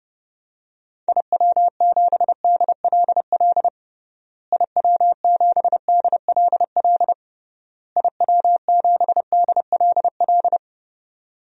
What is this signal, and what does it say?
Morse code, heard as a keyed tone, about 29 wpm, 700 Hz. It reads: IW7DLL IW7DLL IW7DLL
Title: DXCC-100 Call Signs - 6 Characters - Repeated 3x - 30wpm